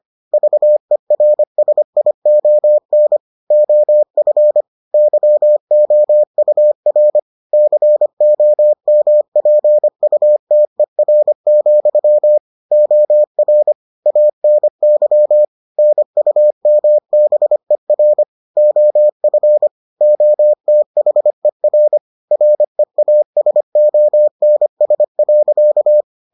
Morse code, heard as a keyed tone, about 25 wpm, 600 Hz. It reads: VERSION OF YOUR COMPUTER, OR ANY NUMBER OF OTHER REASONS.